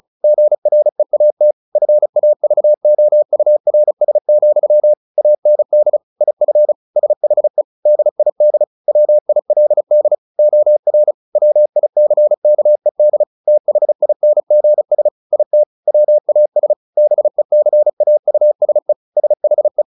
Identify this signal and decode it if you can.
Morse code, heard as a keyed tone, about 34 wpm, 600 Hz. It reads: GREAT FAVOURS, AND IF SHE DID WILD OR WICKED THINGS IT WAS BECAUSE SHE
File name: the old man and the sea / the old man and the sea 1 35wpm